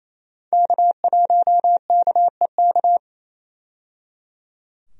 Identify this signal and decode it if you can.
Morse code, heard as a keyed tone, about 27 wpm, 700 Hz. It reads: K1KEK
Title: USA Call Signs - 5 Characters - 28wpm